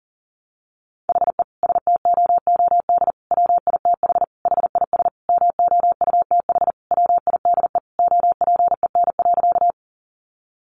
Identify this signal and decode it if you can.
Morse code, heard as a keyed tone, about 40 wpm, 700 Hz. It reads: HE STOOD WITH HIS MOUTH WIDE OPEN.
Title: Sentences from Top 1500 Words 40wpm